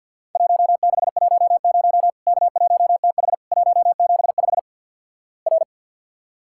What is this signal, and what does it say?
Morse code, heard as a keyed tone, about 50 wpm, 700 Hz. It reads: 1610 K1TH 175 R
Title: Worked All Europe DX Contest - QTC Exchange - 50wpm